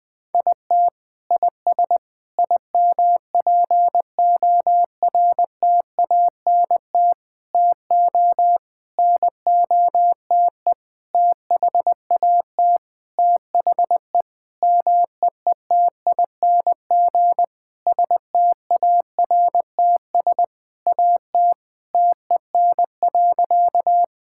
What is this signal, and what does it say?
Morse code, heard as a keyed tone, about 20 wpm, 700 Hz. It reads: IT IS IMPORTANT TO NOTE THAT THE MEETING STARTS AT TEN.
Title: Sentences with 7-Word Phrases - 20wpm